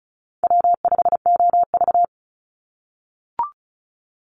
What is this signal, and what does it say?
Morse code, heard as a keyed tone, about 34 wpm, 700 Hz. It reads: W5OV
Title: Call Signs - 35wpm